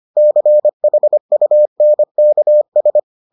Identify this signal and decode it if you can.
Morse code, heard as a keyed tone, about 25 wpm, 600 Hz. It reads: CHUNKS